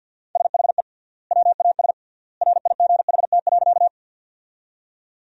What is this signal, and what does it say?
Morse code, heard as a keyed tone, about 52 wpm, 700 Hz. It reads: SHE WAS RIGHT.
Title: 3x Word-Spacing - Sentences from Top 200 Words - 50wpm